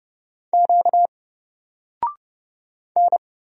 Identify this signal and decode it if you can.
Morse code, heard as a keyed tone, about 31 wpm, 700 Hz. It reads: Q N
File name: Single-Letter-Number-30wpm